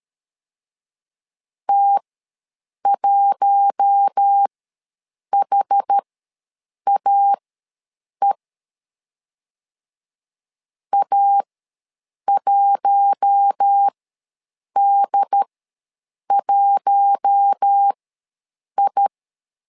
Morse code, heard as a keyed tone, about 13 wpm, spaced as for 7 wpm, 800 Hz. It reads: T1HAE A1D1I
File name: LEZIONI / 4 Q1 ws 12wpm 2129